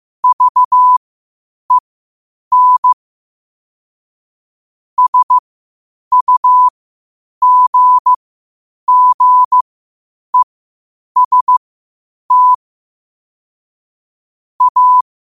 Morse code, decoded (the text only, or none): VEN SUGGEST A